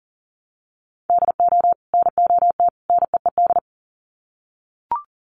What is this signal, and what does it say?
Morse code, heard as a keyed tone, about 40 wpm, 700 Hz. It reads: DO NOT NEED
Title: Sets of 3 Words 40wpm